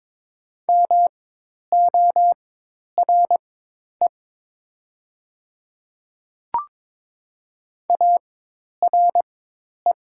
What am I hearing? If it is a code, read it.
Morse code, heard as a keyed tone, about 22 wpm, spaced as for 10 wpm, 700 Hz. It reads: MORE ARE
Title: 4x Character Spacing - Top 100 Words - 22wpm